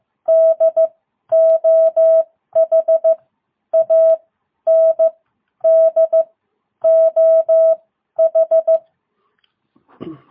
Morse code, heard as a keyed tone, about 15 wpm, spaced as for 11 wpm, 660 Hz. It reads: DOH AND OH